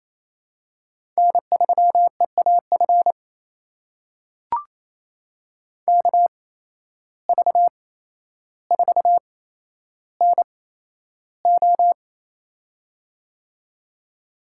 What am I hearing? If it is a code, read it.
Morse code, heard as a keyed tone, about 28 wpm, spaced as for 12 wpm, 700 Hz. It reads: N3EAF KV4NO
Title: USA Call Signs - 5 Characters - 8x Character Spacing - 28wpm